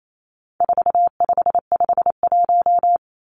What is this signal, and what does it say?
Morse code, heard as a keyed tone, about 28 wpm, 700 Hz. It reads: 4551
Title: Numbers - 4-Digit 28wpm